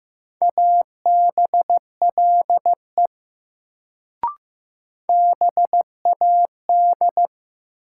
Morse code, heard as a keyed tone, about 15 wpm, 700 Hz. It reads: ABLE BAD
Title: Sentences from Top 700 Words - New Word Drill - 15wpm